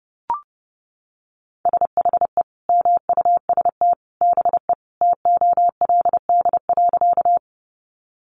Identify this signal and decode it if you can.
Morse code, heard as a keyed tone, about 28 wpm, 700 Hz. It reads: SHE MUST BE TOLD.
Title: Sentences from Top 400 Words 30wpm